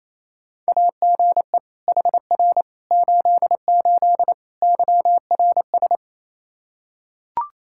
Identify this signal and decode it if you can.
Morse code, heard as a keyed tone, about 28 wpm, 700 Hz. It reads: AGE HR 88 YRS
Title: Practice QSO - Exchange 3 - More Info - 28wpm